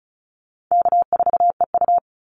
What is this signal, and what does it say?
Morse code, heard as a keyed tone, about 34 wpm, 700 Hz. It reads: K4EU
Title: Easy Call Signs Repeated 2x - 35wpm